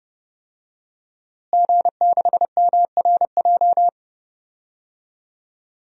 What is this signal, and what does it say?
Morse code, heard as a keyed tone, about 32 wpm, 700 Hz. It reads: G6MRJ
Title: DXCC-100 Call Signs - 5 Characters - Repeated 3x - 30wpm